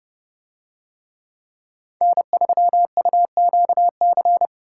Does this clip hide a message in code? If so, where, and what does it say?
Yes: Morse code from 2.0 to 4.5 s, heard as a keyed tone, about 29 wpm, 700 Hz. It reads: N3UQC